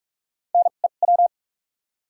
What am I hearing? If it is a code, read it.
Morse code, heard as a keyed tone, about 46 wpm, spaced as for 33 wpm, 700 Hz. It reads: NEW
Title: 2x Character Spacing - Top 200 Words - 45wpm